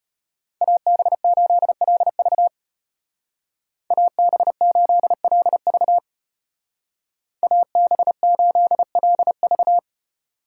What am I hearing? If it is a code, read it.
Morse code, heard as a keyed tone, about 32 wpm, 700 Hz. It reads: AB8LV AB8LV AB8LV